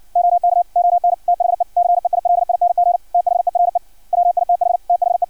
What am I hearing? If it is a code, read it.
Morse code, heard as a keyed tone, about 56 wpm, 700 Hz. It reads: OK ON THE CEILING THERE WITH THE
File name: archived / CRSnet